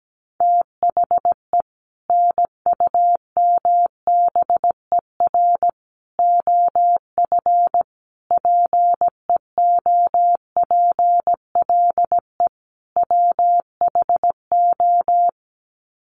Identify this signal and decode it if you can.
Morse code, heard as a keyed tone, about 17 wpm, 700 Hz. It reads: THE NUMBER OF PEOPLE WHO
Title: Sets of 5 Words Encore 17wpm